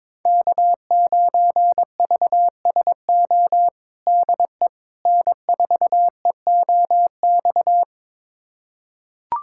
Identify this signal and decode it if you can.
Morse code, heard as a keyed tone, about 22 wpm, 700 Hz. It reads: K9VSO DE N4EOX
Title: USA Callsign DE Callsign - 5 Characters - 22wpm